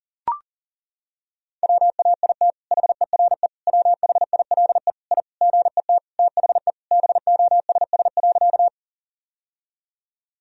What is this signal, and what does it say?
Morse code, heard as a keyed tone, about 40 wpm, 700 Hz. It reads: WAIT HERE WHILE I GET THE BOSS.
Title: Sentences from Top 1000 Words - Review Part 4 - 40wpm